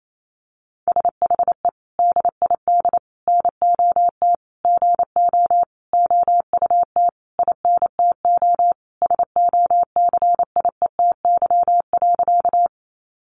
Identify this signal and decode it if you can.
Morse code, heard as a keyed tone, about 28 wpm, 700 Hz. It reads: SHE DID NOT GO OUT INTO SOCIETY.
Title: Sentences from Top 700 Words 28wpm